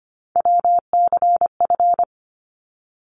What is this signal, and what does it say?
Morse code, heard as a keyed tone, about 25 wpm, 700 Hz. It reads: WCF
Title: ARRL Sections 25wpm